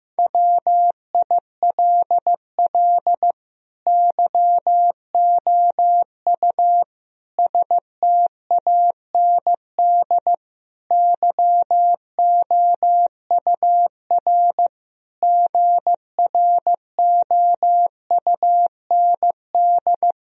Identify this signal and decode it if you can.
Morse code, heard as a keyed tone, about 15 wpm, 700 Hz. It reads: WILL YOU STAND YOUR GROUND
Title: Sentences from Top 1000 Words - Review Part 2 - 15wpm